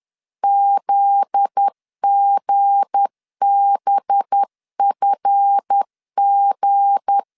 Morse code, heard as a keyed tone, about 10 wpm, 800 Hz. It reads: ZGBFG
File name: CW / DET / 1 9 1500C S A 10wpm 4511